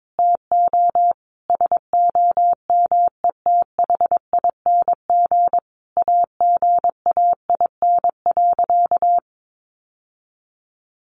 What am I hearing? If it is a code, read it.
Morse code, heard as a keyed tone, about 22 wpm, 700 Hz. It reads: TO SOMETHING AGAIN.
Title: Sentences from Top 1100 Words 22wpm